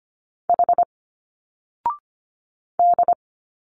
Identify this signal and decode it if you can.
Morse code, heard as a keyed tone, about 25 wpm, 700 Hz. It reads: H D
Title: Single Letters 25wpm